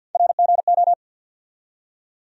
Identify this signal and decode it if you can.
Morse code, heard as a keyed tone, about 46 wpm, 700 Hz. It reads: RGO
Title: USA Call Sign Postfixes - 3 Letters 50wpm